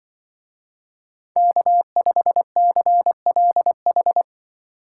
Morse code, heard as a keyed tone, about 25 wpm, 700 Hz. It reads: K5CLH
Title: Speed Racing - USA 1x3 Call Signs 20wpm